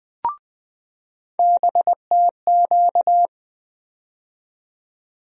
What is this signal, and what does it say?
Morse code, heard as a keyed tone, about 20 wpm, 700 Hz. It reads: BTQ
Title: USA Call Sign Postfixes - 3 Letters 20wpm